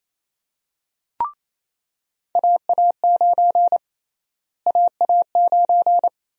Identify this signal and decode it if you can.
Morse code, heard as a keyed tone, about 28 wpm, 700 Hz. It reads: AA9 AA9
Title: USA Call Sign Prefixes - Repeated 2x - 28wpm